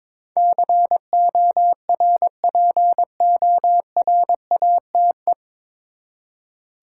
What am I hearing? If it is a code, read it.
Morse code, heard as a keyed tone, about 22 wpm, 700 Hz. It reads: CORPORATE
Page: Morse Code Ninja practice 22wpm